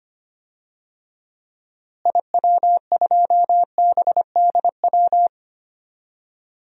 Morse code, heard as a keyed tone, about 25 wpm, 700 Hz. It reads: IW2BDW
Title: DXCC-100 Call Signs - 6 Characters - Repeated 3x - 25wpm